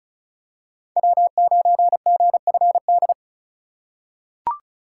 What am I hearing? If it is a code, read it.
Morse code, heard as a keyed tone, about 36 wpm, 700 Hz. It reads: W9GFD
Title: USA Call Signs - 5 Characters - 4x Character Spacing - 35wpm